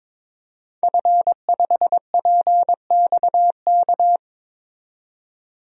Morse code, heard as a keyed tone, about 22 wpm, 700 Hz. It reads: F5PXK
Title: DXCC-100 Call Signs - 5 Characters - 22wpm